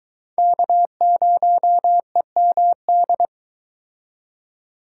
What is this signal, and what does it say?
Morse code, heard as a keyed tone, about 22 wpm, 700 Hz. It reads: K0EMD